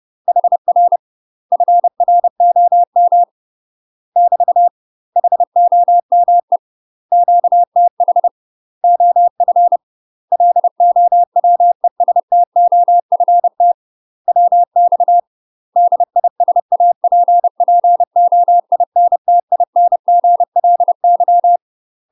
Morse code, heard as a keyed tone, about 30 wpm, 700 Hz. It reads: HR FROM <BT> HOME QTH OF LOWESTOFT WX DISAPPOINTINGLY
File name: UK MASTER QSOs TRAINING / 30WPM / UK MASTER QSOs 0008